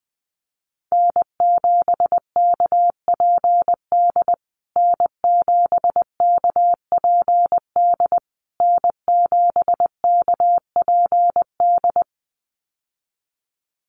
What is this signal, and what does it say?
Morse code, heard as a keyed tone, about 20 wpm, 700 Hz. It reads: N7KPD N7KPD N7KPD
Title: USA Call Signs Repeated 3x 20wpm